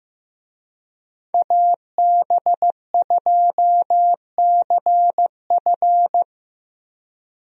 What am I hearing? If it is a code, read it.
Morse code, heard as a keyed tone, about 16 wpm, 700 Hz. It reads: AB2CF
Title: USA Call Signs - 5 Characters - 15wpm